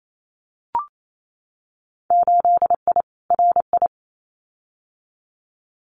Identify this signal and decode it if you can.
Morse code, heard as a keyed tone, about 28 wpm, 700 Hz. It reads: 8I RI
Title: Winter Field Day Exchanges Encore 28wpm